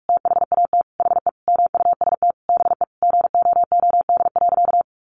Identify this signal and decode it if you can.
Morse code, heard as a keyed tone, about 44 wpm, 700 Hz. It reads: THAT HE MUST BE GOOD.
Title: Sentences from Top 1500 Words 45wpm